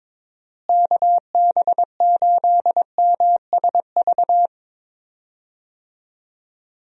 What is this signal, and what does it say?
Morse code, heard as a keyed tone, about 22 wpm, 700 Hz. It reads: KB8MSV
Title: USA Call Signs - 6 Characters - Repeated 2x - 22wpm